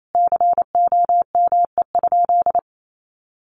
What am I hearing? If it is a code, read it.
Morse code, heard as a keyed tone, about 28 wpm, 700 Hz. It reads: COME?
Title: Sentences from Top 1400 Words 28wpm